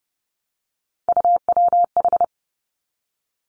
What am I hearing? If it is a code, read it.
Morse code, heard as a keyed tone, about 32 wpm, 700 Hz. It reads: UWH
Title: Three Challenging Letter-Number Combinations 30wpm